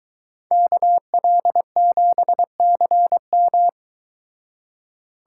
Morse code, heard as a keyed tone, about 22 wpm, 700 Hz. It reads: KL7CM